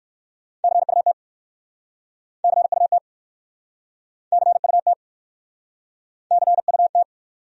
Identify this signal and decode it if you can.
Morse code, heard as a keyed tone, about 59 wpm, 700 Hz. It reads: CUT CUT CUT CUT